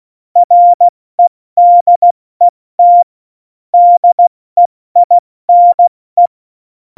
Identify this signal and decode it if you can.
Morse code, heard as a keyed tone, about 16 wpm, 700 Hz. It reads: REDET DEINE
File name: Morsekurs / Witze / Witz 23. Juli 2019, 70 F80